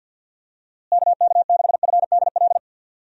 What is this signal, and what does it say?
Morse code, heard as a keyed tone, about 51 wpm, 700 Hz. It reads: KK6FDL
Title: USA Call Signs - 6 Characters - Repeated 3x - 50wpm